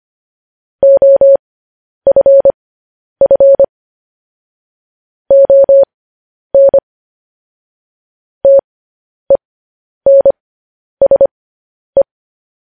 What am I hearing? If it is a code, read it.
Morse code, heard as a keyed tone, about 25 wpm, spaced as for 10 wpm, 550 Hz. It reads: OFF ON TENSE